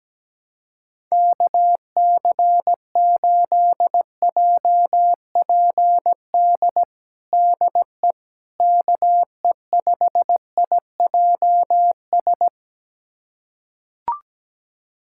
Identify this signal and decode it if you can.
Morse code, heard as a keyed tone, about 17 wpm, 700 Hz. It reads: KC8JPD DE KE5IJS
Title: USA Callsign DE Callsign - 6 Characters - 17wpm